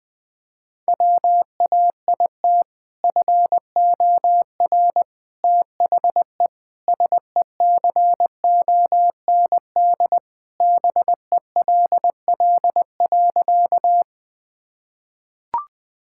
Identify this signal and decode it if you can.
Morse code, heard as a keyed tone, about 20 wpm, 700 Hz. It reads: WAIT FOR THE SECOND BELL.
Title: Sentences with 2-Word Phrases - 20wpm